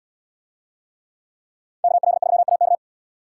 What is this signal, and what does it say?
Morse code, heard as a keyed tone, about 75 wpm, 700 Hz. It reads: KL3AK